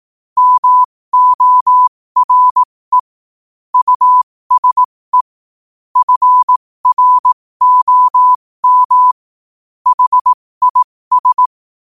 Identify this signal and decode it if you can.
Morse code, heard as a keyed tone, about 18 wpm, spaced as for 15 wpm, 1000 Hz. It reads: MORE USE FROM HIS
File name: qotdfiles / 15wpm